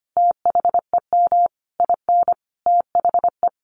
Morse code, heard as a keyed tone, about 25 wpm, 700 Hz. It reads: THEM IN THE
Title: A Sound Of Thunder 25wpm